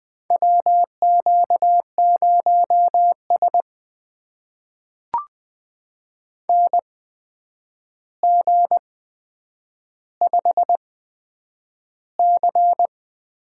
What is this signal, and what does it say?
Morse code, heard as a keyed tone, about 20 wpm, spaced as for 9 wpm, 700 Hz. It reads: WQ0S NG5C
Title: USA Call Signs - 4 Characters - 8x Character Spacing - 20wpm